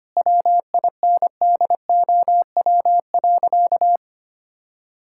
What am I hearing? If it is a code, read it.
Morse code, heard as a keyed tone, about 25 wpm, 700 Hz. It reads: WINDOW.
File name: practice / 25wpm / Sentences with 2-Word Phrases - 25wpm